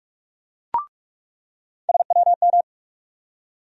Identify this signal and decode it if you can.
Morse code, heard as a keyed tone, about 46 wpm, 700 Hz. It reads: SWM
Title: USA Call Sign Postfixes - 3 Letters 45wpm